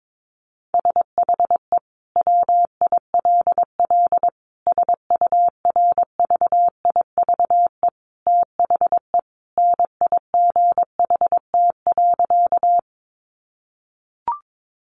Morse code, heard as a keyed tone, about 22 wpm, 700 Hz. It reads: SHE WILL SURVIVE THE NIGHT.
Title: Sentences from Top 1300 Words 22wpm